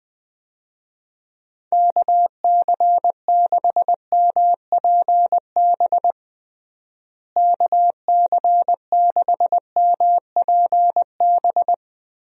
Morse code, heard as a keyed tone, about 20 wpm, 700 Hz. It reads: KC6MPB KC6MPB
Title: USA Call Signs - 6 Characters - Repeated 3x - 20wpm